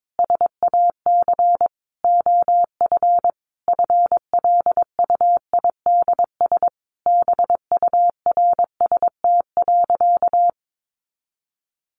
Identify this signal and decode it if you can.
Morse code, heard as a keyed tone, about 22 wpm, 700 Hz. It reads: SAC OF FLUIDS BURST.
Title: A Sound Of Thunder 22wpm